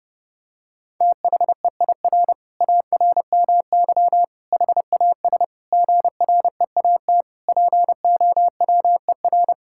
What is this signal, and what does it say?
Morse code, heard as a keyed tone, about 30 wpm, 700 Hz. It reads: THEIR ARMY HAS GREAT POWER
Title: Sentences from Top 1000 Words - Review Part 5 - 30wpm